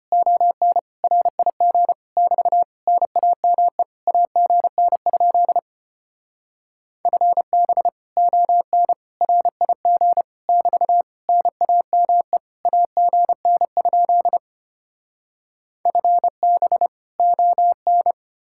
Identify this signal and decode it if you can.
Morse code, heard as a keyed tone, about 33 wpm, 700 Hz. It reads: ON RIG <BT> NAME AGN? FB ON RIG <BT> NAME AGN? FB ON